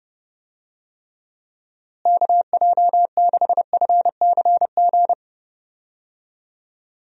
Morse code, heard as a keyed tone, about 30 wpm, 700 Hz. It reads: KJ6FCG
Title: USA Call Signs - 6 Characters - Repeated 3x - 30wpm